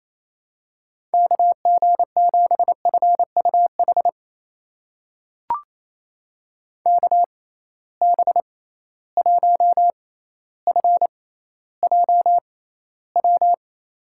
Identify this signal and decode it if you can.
Morse code, heard as a keyed tone, about 28 wpm, spaced as for 14 wpm, 700 Hz. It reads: KG7FUH KB1FJW
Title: USA Call Signs - 6 Characters - 6x Character Spacing - 28wpm